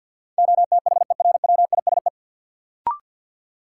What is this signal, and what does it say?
Morse code, heard as a keyed tone, about 53 wpm, 700 Hz. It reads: OTHERWISE
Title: Sentences from Top 1200 Words - New Word Drill - 50wpm